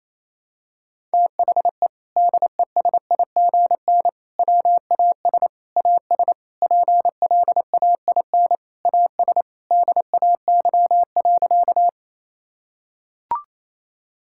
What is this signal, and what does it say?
Morse code, heard as a keyed tone, about 28 wpm, 700 Hz. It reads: THE DESIGN WAS AS PLAIN AS DAY.
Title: Sentences from Top 1000 Words - Review Part 5 - 28wpm